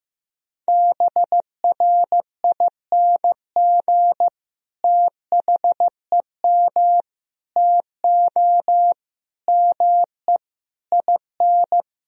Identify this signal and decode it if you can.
Morse code, heard as a keyed tone, about 15 wpm, 700 Hz. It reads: BRING THEM TO ME IN
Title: Sentences from Top 1000 Words - Review Part 3 - 15wpm